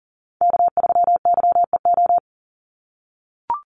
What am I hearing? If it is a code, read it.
Morse code, heard as a keyed tone, about 40 wpm, 700 Hz. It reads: K3YEO
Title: Call Signs - 40wpm